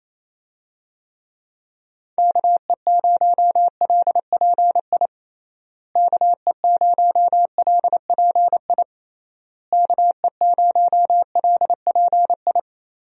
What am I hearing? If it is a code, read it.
Morse code, heard as a keyed tone, about 28 wpm, 700 Hz. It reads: KE0LPI KE0LPI KE0LPI
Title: USA Call Signs - 6 Characters - Repeated 3x - 28wpm